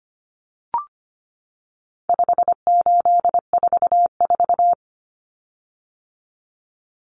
Morse code, heard as a keyed tone, about 25 wpm, 700 Hz. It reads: 5844
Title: Common Cut Numbers - 4-Digit 25wpm